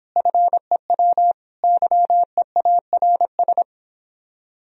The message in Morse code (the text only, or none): FEW YEARS